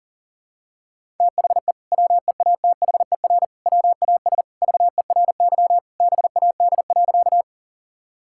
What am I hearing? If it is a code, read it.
Morse code, heard as a keyed tone, about 37 wpm, 700 Hz. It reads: THE WEATHER WAS VERY BAD.
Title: Sentences from Top 1000 Words - Review Part 5 - 40wpm